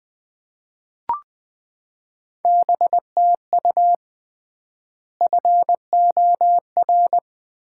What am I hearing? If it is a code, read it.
Morse code, heard as a keyed tone, about 20 wpm, 700 Hz. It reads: BTU FOR
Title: Common QSO Phrases - 3x Word-Spacing - 20wpm